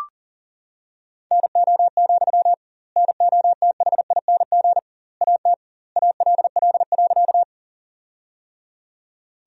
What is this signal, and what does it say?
Morse code, heard as a keyed tone, about 40 wpm, 700 Hz. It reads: NO, NOTHING AT ALL.
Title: 2x Word-Spacing - Sentences from Top 500 Words - 40wpm